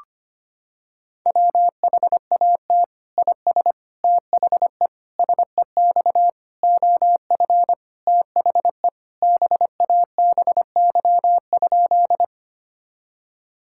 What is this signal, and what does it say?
Morse code, heard as a keyed tone, about 25 wpm, 700 Hz. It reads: WHAT IS THE SEX OF THE BABY?
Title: Sentences from Top 1600 Words 25wpm